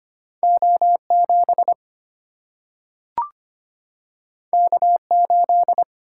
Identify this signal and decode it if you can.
Morse code, heard as a keyed tone, about 25 wpm, 700 Hz. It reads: O7 K8
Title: Single-Letter Single-Number 25wpm